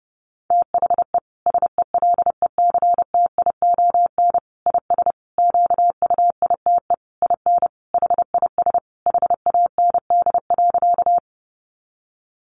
Morse code, heard as a keyed tone, about 30 wpm, 700 Hz. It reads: THE SELECTION IS QUITE IN HIS HAND.
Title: Sentences from Top 1400 Words 30wpm